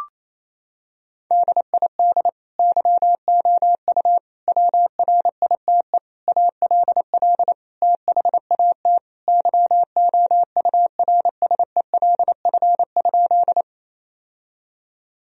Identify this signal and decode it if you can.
Morse code, heard as a keyed tone, about 28 wpm, 700 Hz. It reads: DID YOU WRITE ALL THAT YOURSELF?
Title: Sentences from Top 1000 Words - Review Part 1 - 28wpm